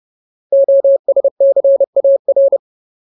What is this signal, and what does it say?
Morse code, heard as a keyed tone, about 30 wpm, 550 Hz. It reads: OSCAR
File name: Ultimate Word List / 30WPM Ultimate